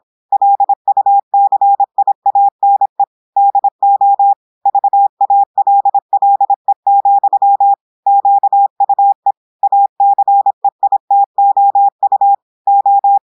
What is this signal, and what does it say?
Morse code, heard as a keyed tone, about 26 wpm, 800 Hz. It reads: LUCIANE DO VALLE, QUE ACEITOU O
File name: audios / portugais / portugais texte en clair 1 vitesse 26 mots minutes